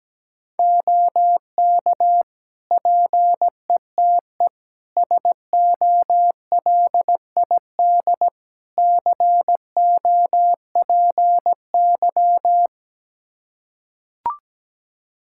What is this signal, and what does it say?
Morse code, heard as a keyed tone, about 17 wpm, 700 Hz. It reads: OK PETE SOLID COPY
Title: Practice QSO - Exchange 3 - More Info - 17wpm